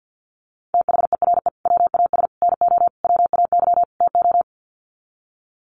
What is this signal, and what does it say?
Morse code, heard as a keyed tone, about 50 wpm, 700 Hz. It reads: THERE WAS NO WAY TO
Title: Sets of 5 Words Encore 50wpm